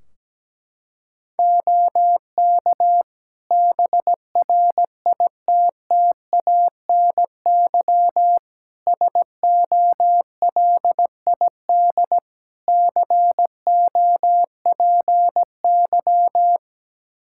Morse code, heard as a keyed tone, about 17 wpm, 700 Hz. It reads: OK BRITTANY SOLID COPY
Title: K1USN SST QSO Practice 17wpm